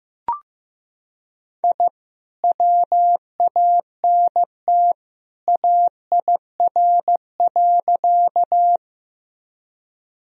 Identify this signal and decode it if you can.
Morse code, heard as a keyed tone, about 15 wpm, 700 Hz. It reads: I WANT AIR.
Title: Sentences from Top 1000 Words - Review Part 1 - 15wpm